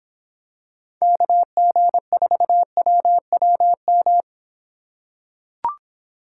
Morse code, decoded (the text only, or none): KG4WWM